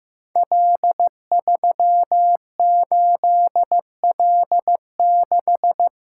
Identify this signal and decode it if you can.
Morse code, heard as a keyed tone, about 16 wpm, 700 Hz. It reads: L38L6
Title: Code Groups - 5 Letter:Numbers - 15wpm